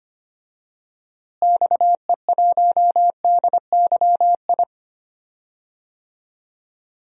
Morse code, heard as a keyed tone, about 25 wpm, 700 Hz. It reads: XE1DYI
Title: DXCC-100 Call Signs - 6 Characters - Repeated 3x - 25wpm